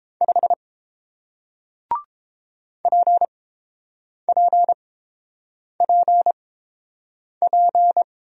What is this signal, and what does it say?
Morse code, heard as a keyed tone, about 30 wpm, spaced as for 7 wpm, 700 Hz. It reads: 5 PPPP